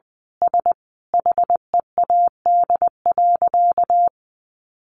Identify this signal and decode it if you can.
Morse code, heard as a keyed tone, about 19 wpm, 700 Hz. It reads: S HEAD.
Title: Easy Phrases 20wpm